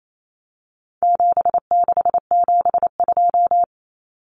Morse code, heard as a keyed tone, about 28 wpm, 700 Hz. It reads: 7672
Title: Common Cut Numbers - 4-Digit 28wpm